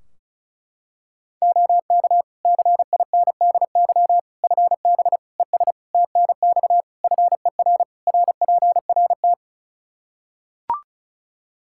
Morse code, heard as a keyed tone, about 34 wpm, 700 Hz. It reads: OK CINDY FB ES TNX FER RPRT
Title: Practice QSO - Exchange 2 - Info - 35wpm